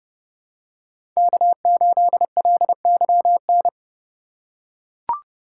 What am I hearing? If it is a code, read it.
Morse code, heard as a keyed tone, about 28 wpm, 700 Hz. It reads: K8LYN